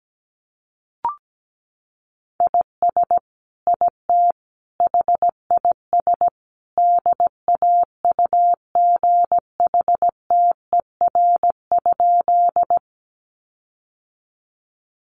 Morse code, heard as a keyed tone, about 17 wpm, 700 Hz. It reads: IS IT HIS DAUGHTER?
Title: Sentences from Top 1500 Words 17wpm